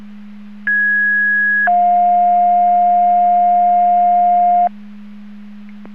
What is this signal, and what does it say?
Two-tone page, heard as a one-second tone, then a three-second tone, about 1650 then 710 Hz. It is heard from 0.7 to 4.7 s.